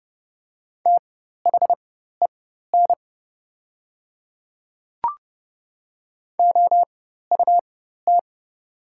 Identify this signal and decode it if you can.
Morse code, heard as a keyed tone, about 30 wpm, spaced as for 14 wpm, 700 Hz. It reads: THEN OUT
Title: 4x Character Spacing - Top 100 Words - 30wpm